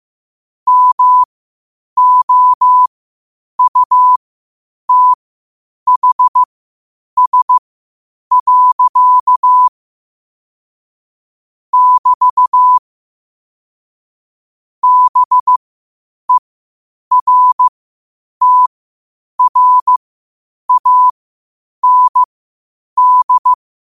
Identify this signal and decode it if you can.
Morse code, heard as a keyed tone, about 15 wpm, spaced as for 8 wpm, 1000 Hz. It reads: MOUTHS. <BT> BERTRAND
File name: qotdfiles / 8wpm